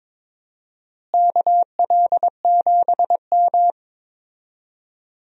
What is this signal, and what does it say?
Morse code, heard as a keyed tone, about 22 wpm, 700 Hz. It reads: KL7M
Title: DXCC-100 Call Signs - 4 Characters - Repeated 2x - 22wpm